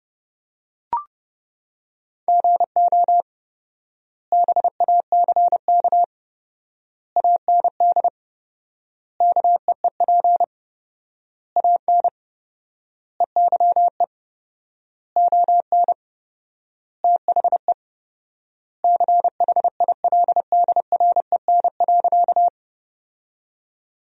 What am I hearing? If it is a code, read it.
Morse code, heard as a keyed tone, about 32 wpm, 700 Hz. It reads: GO BACK AND KEEP AN EYE ON THE CHILDREN.
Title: 4x Word-Spacing - Sentences from Top 300 Words - 30wpm